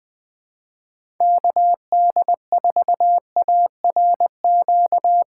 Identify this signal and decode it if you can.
Morse code, heard as a keyed tone, about 20 wpm, 700 Hz. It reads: KD4ARQ
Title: USA Call Signs - 6 Characters - 4x Character Spacing - 20wpm